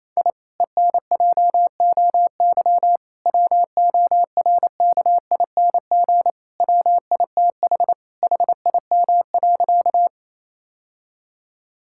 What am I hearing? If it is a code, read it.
Morse code, heard as a keyed tone, about 28 wpm, 700 Hz. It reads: I ENJOY WORKING WITH HIM.
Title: Sentences from Top 1000 Words - Review Part 5 - 28wpm